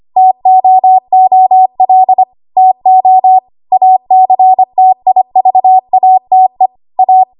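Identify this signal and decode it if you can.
Morse code, heard as a keyed tone, about 25 wpm, 750 Hz. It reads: TOOL TO ACTIVATE A
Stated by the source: ARRL 25 WPM code practice archive.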